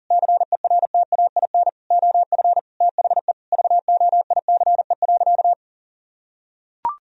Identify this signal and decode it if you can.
Morse code, heard as a keyed tone, about 37 wpm, 700 Hz. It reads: CERTAIN OF THE VOICE.
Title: Sentences from Top 1000 Words - Review Part 2 - 40wpm